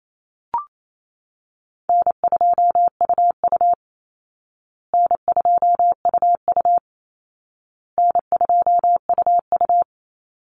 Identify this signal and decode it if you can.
Morse code, heard as a keyed tone, about 28 wpm, 700 Hz. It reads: N2UU N2UU N2UU
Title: Call Signs Repeated 2x - 28wpm